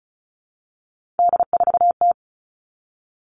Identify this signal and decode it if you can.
Morse code, heard as a keyed tone, about 34 wpm, 700 Hz. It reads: D4T
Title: All Cut Numbers - 3-Digit 35wpm